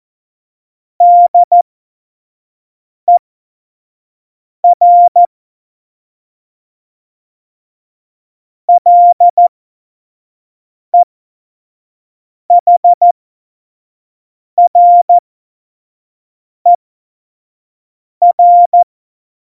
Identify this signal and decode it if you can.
Morse code, heard as a keyed tone, about 14 wpm, spaced as for 5 wpm, 700 Hz. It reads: DER LEHRER